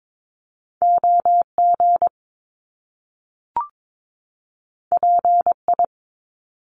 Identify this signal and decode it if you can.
Morse code, heard as a keyed tone, about 22 wpm, 700 Hz. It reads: OG PI
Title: Top Two Letter Combinations in Words 22wpm